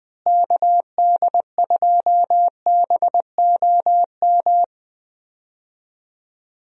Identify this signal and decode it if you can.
Morse code, heard as a keyed tone, about 20 wpm, 700 Hz. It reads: KD2BOM
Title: USA Call Signs - 6 Characters - Repeated 3x - 20wpm